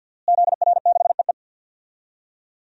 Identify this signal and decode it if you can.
Morse code, heard as a keyed tone, about 46 wpm, 700 Hz. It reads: ZR6EE
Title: DXCC-100 Call Signs - 5 Characters - Repeated 2x - 50wpm